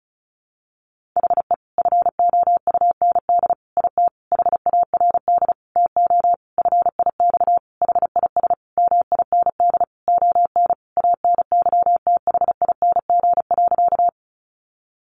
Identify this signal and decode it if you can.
Morse code, heard as a keyed tone, about 34 wpm, 700 Hz. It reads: HE FOUND IT HARD TO FIX HIS MIND ON ANYTHING.
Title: Sentences from Top 1400 Words 35wpm